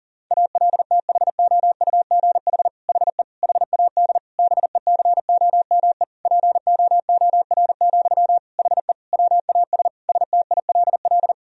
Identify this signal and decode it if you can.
Morse code, heard as a keyed tone, about 40 wpm, 700 Hz. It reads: ALTHOUGH HE HAD BECOME POOR, HE WAS STILL
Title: Sentences from Top 1000 Words - Review Part 3 - 40wpm